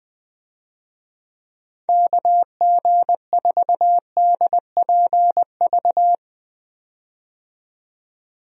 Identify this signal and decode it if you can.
Morse code, heard as a keyed tone, about 20 wpm, 700 Hz. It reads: KG4DPV
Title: USA Call Signs - 6 Characters - 20wpm